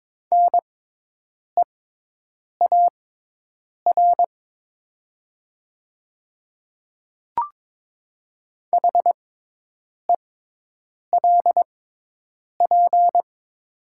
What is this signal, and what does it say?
Morse code, heard as a keyed tone, about 22 wpm, spaced as for 8 wpm, 700 Hz. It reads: NEAR HELP
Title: 6x Character Spacing - Top 200 Words - 22wpm